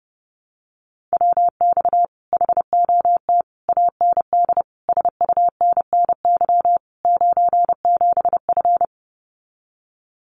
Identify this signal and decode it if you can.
Morse code, heard as a keyed tone, about 30 wpm, 700 Hz. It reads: WX HOT AND SUNNY 97F
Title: Common QSO Phrases Encore 30wpm